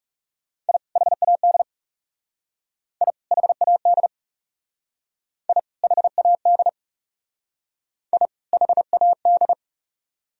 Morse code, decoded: I HAD I HAD I HAD I HAD

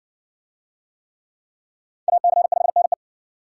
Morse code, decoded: AC5NE

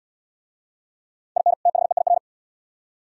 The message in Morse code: IN THEIR

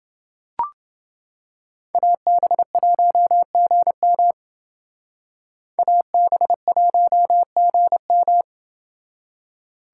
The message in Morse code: AB1GM AB1GM

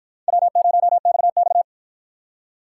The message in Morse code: W0XX

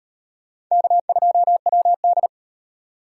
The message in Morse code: K2WD

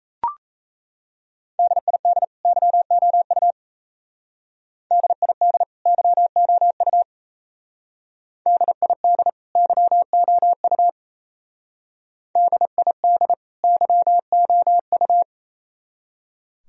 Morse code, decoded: DID YOU DID YOU DID YOU DID YOU

